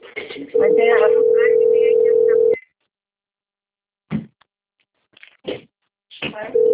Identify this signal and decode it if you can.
Ringback tone (North America), 440 + 480 Hz, with speech.
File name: rg-1600-unknown- - - .25697